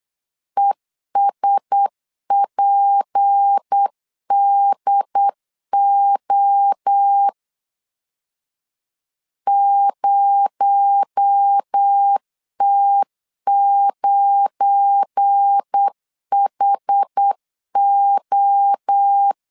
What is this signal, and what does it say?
Morse code, heard as a keyed tone, about 8 wpm, 800 Hz. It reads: ESPDO 0T9HO
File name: LEZIONI / 7 GF9Z / 7 GF9 8wpm 3545